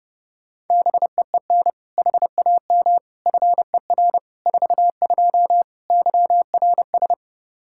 Morse code, heard as a keyed tone, about 32 wpm, 700 Hz. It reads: BEEN HAM FER 42 YRS
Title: Practice QSO - Exchange 3 - More Info - 30wpm